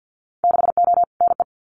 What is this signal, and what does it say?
Morse code, heard as a keyed tone, about 50 wpm, 700 Hz. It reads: 6O NE
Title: Winter Field Day Exchanges Encore 50wpm